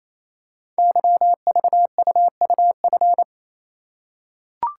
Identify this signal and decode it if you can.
Morse code, heard as a keyed tone, about 27 wpm, 700 Hz. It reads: YVUUF
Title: Code Groups - 5 Letter:Numbers - 28wpm